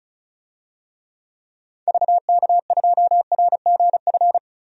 Morse code, heard as a keyed tone, about 34 wpm, 700 Hz. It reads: VK2RGF
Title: DXCC-100 Call Signs - 6 Characters - 35wpm